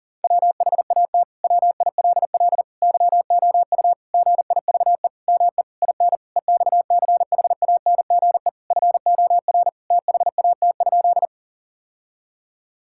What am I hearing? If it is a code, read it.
Morse code, heard as a keyed tone, about 40 wpm, 700 Hz. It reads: WHAT WILL YOU GIVE ME IN EXCHANGE FOR THAT?
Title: Sentences from Top 1000 Words - Review Part 6 - 40wpm